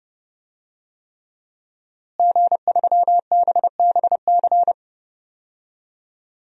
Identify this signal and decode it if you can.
Morse code, heard as a keyed tone, about 30 wpm, 700 Hz. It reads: G3BBC
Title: DXCC-100 Call Signs - 5 Characters - Repeated 3x - 30wpm